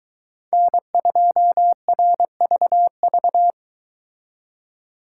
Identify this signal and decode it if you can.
Morse code, heard as a keyed tone, about 23 wpm, 700 Hz. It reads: N2RVV